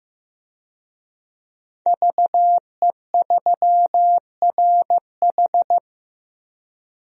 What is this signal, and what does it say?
Morse code, heard as a keyed tone, about 15 wpm, 700 Hz. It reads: VE3RH
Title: DXCC-100 Call Signs - 5 Characters - Repeated 2x - 15wpm